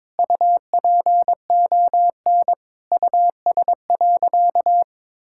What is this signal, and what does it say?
Morse code, heard as a keyed tone, about 22 wpm, 700 Hz. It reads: UPON US.
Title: Sentences from Top 1600 Words 22wpm